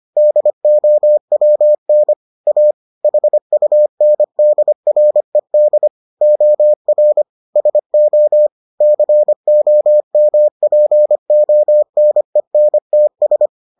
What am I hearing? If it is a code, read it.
Morse code, heard as a keyed tone, about 25 wpm, 600 Hz. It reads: DOWN A HUNDRED OR SO COMPONENTS